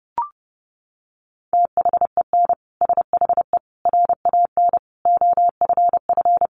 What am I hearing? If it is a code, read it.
Morse code, heard as a keyed tone, about 30 wpm, 700 Hz. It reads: THEN SHE RAN OFF
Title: Sentences from Top 1400 Words 30wpm